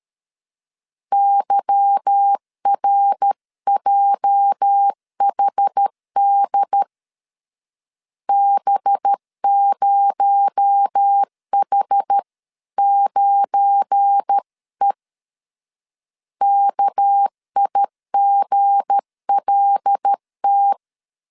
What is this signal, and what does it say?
Morse code, heard as a keyed tone, about 13 wpm, 800 Hz. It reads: YRJHD B0H9E KIGLT
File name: LEZIONI / 8 YJ45 / 8 YJ 12wpm 2235